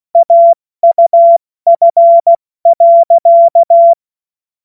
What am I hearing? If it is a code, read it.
Morse code, heard as a keyed tone, about 16 wpm, 680 Hz. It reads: AUF.
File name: Morsekurs / Witze / Witz 21. Dezember 2021, 70 F80